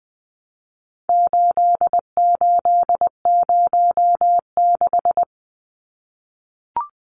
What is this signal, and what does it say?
Morse code, heard as a keyed tone, about 20 wpm, 700 Hz. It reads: 8806
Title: Numbers - 4-Digit 20wpm